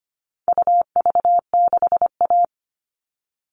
Morse code, heard as a keyed tone, about 25 wpm, 700 Hz. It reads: UV6A
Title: All Cut Numbers - 4-Digit 25wpm